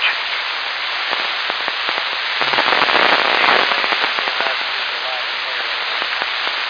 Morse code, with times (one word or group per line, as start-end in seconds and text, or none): none